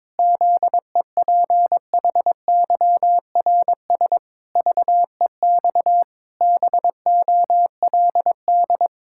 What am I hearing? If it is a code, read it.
Morse code, heard as a keyed tone, about 22 wpm, 700 Hz. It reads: ZEPHYRS VEX BOLD
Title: Pangram Challenge v2 - 22wpm